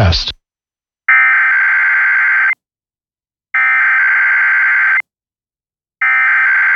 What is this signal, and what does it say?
Emergency Alert System (SAME) tones (header bursts), heard as harsh buzzing bursts, with speech. It is heard from 1.1 to 6.0 s.